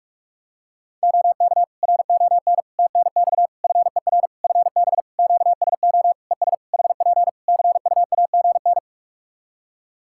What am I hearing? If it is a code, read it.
Morse code, heard as a keyed tone, about 42 wpm, 700 Hz. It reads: OK RON TNX FER FB QSO ES HP CUAGN
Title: Practice QSO - Exchange 4 - Ending - 45wpm